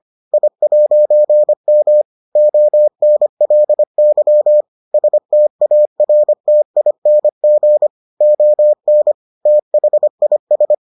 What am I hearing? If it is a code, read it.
Morse code, heard as a keyed tone, about 25 wpm, 600 Hz. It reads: I'M ONLY STARTING ON THIS